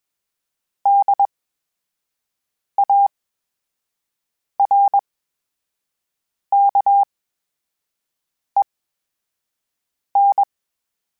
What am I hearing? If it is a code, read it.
Morse code, heard as a keyed tone, about 20 wpm, spaced as for 5 wpm, 800 Hz. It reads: DARKEN
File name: Special INGs-mp3s / Special INGs-mp3s / VerySlowToFaster Ing @20